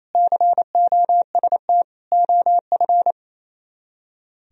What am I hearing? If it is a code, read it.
Morse code, heard as a keyed tone, about 33 wpm, 700 Hz. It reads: COST OF
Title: Speed-Racing - Sets of 3 Words 28wpm